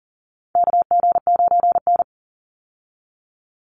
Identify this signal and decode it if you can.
Morse code, heard as a keyed tone, about 40 wpm, 700 Hz. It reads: KG9N
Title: Call Signs Repeated 3x - 40wpm